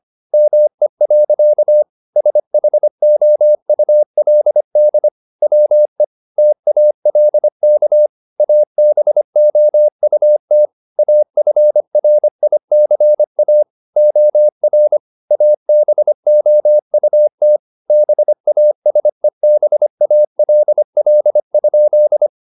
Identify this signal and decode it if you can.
Morse code, heard as a keyed tone, about 25 wpm, 600 Hz. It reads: ME. SHOULD WE TALK ABOUT AFRICA OR ABOUT BASEBALL?
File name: the old man and the sea part 1 25wpm